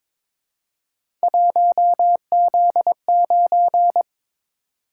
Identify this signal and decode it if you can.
Morse code, heard as a keyed tone, about 22 wpm, 700 Hz. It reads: 1Z9
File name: practice / 22wpm / Three Letter-Number Combinations - 22wpm